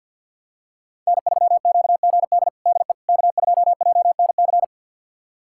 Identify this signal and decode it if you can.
Morse code, heard as a keyed tone, about 49 wpm, 700 Hz. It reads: N3QGD DE K2JNC
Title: USA Callsign DE Callsign - 5 Characters - 50wpm